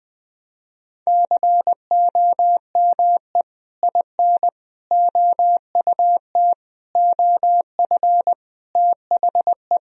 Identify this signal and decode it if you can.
Morse code, heard as a keyed tone, about 20 wpm, 700 Hz. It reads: COME IN OUT OF THE
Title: Sentences from Top 1000 Words - Review Part 1 - 20wpm